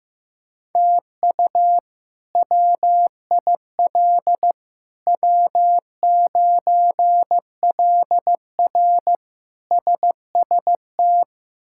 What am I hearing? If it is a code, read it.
Morse code, heard as a keyed tone, about 15 wpm, 700 Hz. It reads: TU WIL W9LR SST
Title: K1USN SST QSO Practice - 15wpm